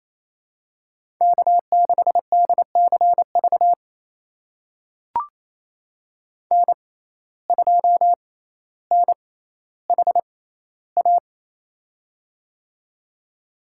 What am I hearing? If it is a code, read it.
Morse code, heard as a keyed tone, about 28 wpm, spaced as for 14 wpm, 700 Hz. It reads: K6DCV N2NHA
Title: USA Call Signs - 5 Characters - 6x Character Spacing - 28wpm